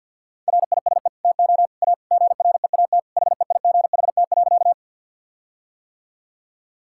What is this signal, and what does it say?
Morse code, heard as a keyed tone, about 47 wpm, 700 Hz. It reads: RISE TO A GREAT HEIGHT.
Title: Sentences from Top 1600 Words 50wpm